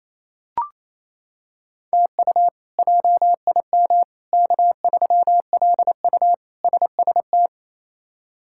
Morse code, heard as a keyed tone, about 28 wpm, 700 Hz. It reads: TU JIM K3LU SST
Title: K1USN SST QSO Practice - 28wpm